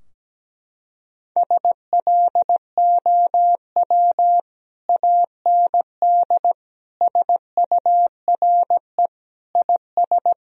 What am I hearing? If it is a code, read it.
Morse code, heard as a keyed tone, about 17 wpm, 700 Hz. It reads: SLOW AND SURE IS
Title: Sentences from Top 1000 Words - Review Part 2 - 17wpm